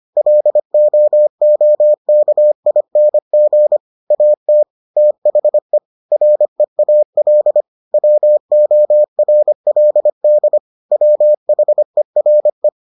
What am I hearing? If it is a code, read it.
Morse code, heard as a keyed tone, about 25 wpm, 600 Hz. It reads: LOOKING AT THE REAL WORLD WHERE